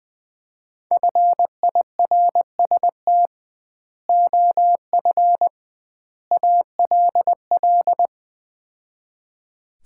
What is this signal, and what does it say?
Morse code, heard as a keyed tone, about 20 wpm, 700 Hz. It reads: FIRST OF ALL
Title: 2x Word-Spacing - Sets of 3 Words - 20wpm